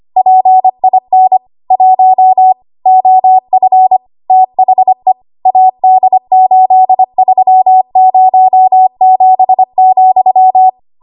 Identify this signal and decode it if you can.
Morse code, heard as a keyed tone, about 25 wpm, 750 Hz. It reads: PIN 1 OF THE AD8307,
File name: Archive / 25 WPM / 25WPM